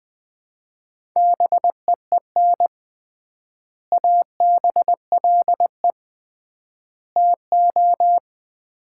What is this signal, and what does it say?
Morse code, heard as a keyed tone, about 20 wpm, 700 Hz. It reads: BEEN ABLE TO
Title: 3x Word-Spacing - Sets of 3 Words - 20wpm